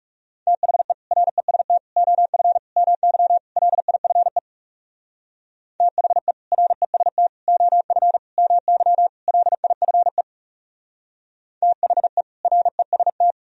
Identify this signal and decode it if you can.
Morse code, heard as a keyed tone, about 37 wpm, 700 Hz. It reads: THE REST OF MY LIFE THE REST OF MY LIFE THE REST
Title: Speed-Racing - Sets of 5 Words 30wpm